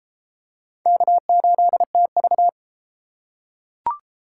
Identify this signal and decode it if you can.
Morse code, heard as a keyed tone, about 28 wpm, 700 Hz. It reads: K8TV
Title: Speed Racing - USA 1x2 Call Signs 22wpm